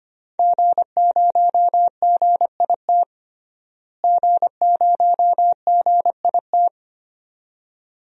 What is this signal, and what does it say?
Morse code, heard as a keyed tone, about 25 wpm, 700 Hz. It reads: G0GIT G0GIT
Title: DXCC-100 Call Signs - 5 Characters - Repeated 2x - 25wpm